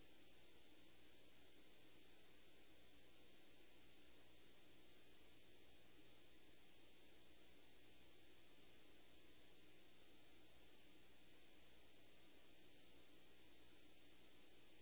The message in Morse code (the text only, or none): none